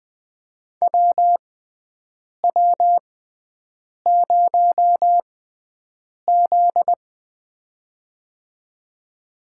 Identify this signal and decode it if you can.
Morse code, heard as a keyed tone, about 20 wpm, spaced as for 7 wpm, 700 Hz. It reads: WW0Z
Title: USA Call Signs - 4 Characters - 6x Character Spacing - 20wpm